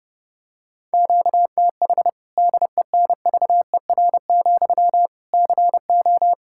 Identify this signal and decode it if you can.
Morse code, heard as a keyed tone, about 32 wpm, 700 Hz. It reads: QTH DENVER, CO
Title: K1USN SST QSO Practice 30wpm